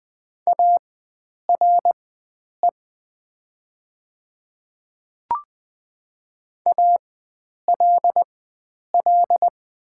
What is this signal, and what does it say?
Morse code, heard as a keyed tone, about 20 wpm, spaced as for 9 wpm, 690 Hz. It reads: ARE ALL
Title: 4x Character Spacing - Top 100 Words - 20wpm